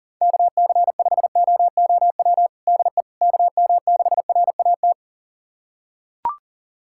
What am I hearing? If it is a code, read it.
Morse code, heard as a keyed tone, about 38 wpm, 700 Hz. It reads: KC5OOW DE KM6RAT
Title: USA Callsign DE Callsign - 6 Characters - 40wpm